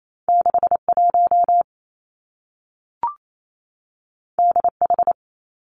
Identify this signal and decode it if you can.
Morse code, heard as a keyed tone, about 28 wpm, 700 Hz. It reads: 6 1 D H
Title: Two Challenging Letter-Number Combinations 28wpm